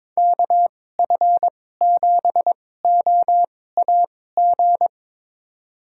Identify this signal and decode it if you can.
Morse code, heard as a keyed tone, about 22 wpm, spaced as for 16 wpm, 700 Hz. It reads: KF7OAG
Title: USA Call Signs - 6 Characters - 2x Character Spacing - 22wpm